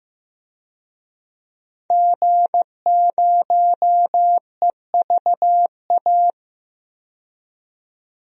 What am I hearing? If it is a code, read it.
Morse code, heard as a keyed tone, about 15 wpm, 700 Hz. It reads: G0EVA
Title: DXCC-100 Call Signs - 5 Characters - 15wpm